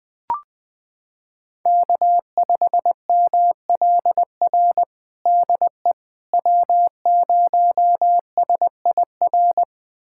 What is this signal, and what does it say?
Morse code, heard as a keyed tone, about 21 wpm, 700 Hz. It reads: K5MLR DE W0SIR
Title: USA Callsign DE Callsign - 5 Characters - 20wpm